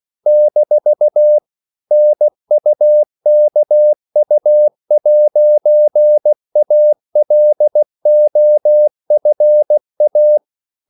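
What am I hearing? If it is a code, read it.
Morse code, heard as a keyed tone, about 16 wpm, 600 Hz. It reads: - NUKU'ALOFA
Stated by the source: CW practice of national capitals at 16 words per minute, from ON6ZQ.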